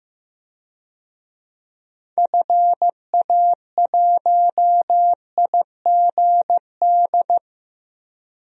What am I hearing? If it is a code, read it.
Morse code, heard as a keyed tone, about 16 wpm, 700 Hz. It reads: FA1IGD
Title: DXCC-100 Call Signs - 6 Characters - 15wpm